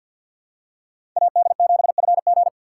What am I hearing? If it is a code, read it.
Morse code, heard as a keyed tone, about 53 wpm, 700 Hz. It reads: AD7FG
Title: USA Call Signs - 5 Characters - 3x Character Spacing - 50wpm